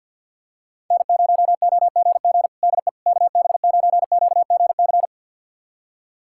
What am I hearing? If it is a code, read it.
Morse code, heard as a keyed tone, about 52 wpm, 700 Hz. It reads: N0OGG DE KB9QGC
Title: USA Callsign DE Callsign 50wpm